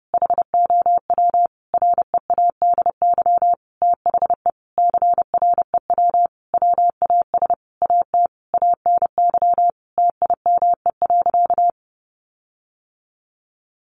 Morse code, heard as a keyed tone, about 30 wpm, 700 Hz. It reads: HOW READY THE CREW WAS AT ANY TIME.
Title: Sentences from Top 1500 Words 30wpm